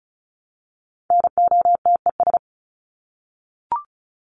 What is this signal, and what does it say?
Morse code, heard as a keyed tone, about 34 wpm, 700 Hz. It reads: NOTES
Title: ES:S Suffix - Top 100 Words with Base Words 35wpm